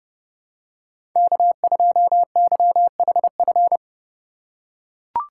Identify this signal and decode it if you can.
Morse code, heard as a keyed tone, about 30 wpm, 700 Hz. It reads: K2YHF
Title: USA Call Signs - 5 Characters - 2x Character Spacing - 30wpm